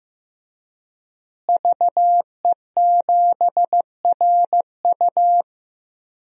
Morse code, heard as a keyed tone, about 15 wpm, 700 Hz. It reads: VE7RU
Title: DXCC-100 Call Signs - 5 Characters - Repeated 2x - 15wpm